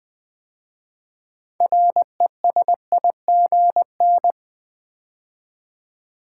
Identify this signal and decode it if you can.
Morse code, heard as a keyed tone, about 20 wpm, 700 Hz. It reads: RESIGN
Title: 6 Letter Words - First 3 Letters Given - 20wpm